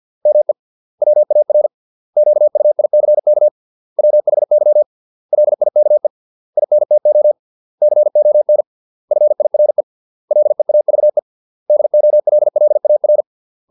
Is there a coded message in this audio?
Yes: Morse code from 0.3 to 13.2 s, heard as a keyed tone, about 50 wpm, 600 Hz. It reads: ME WAR QUICK WHY LIKE INTO CON FIRE LEAVE DOLLAR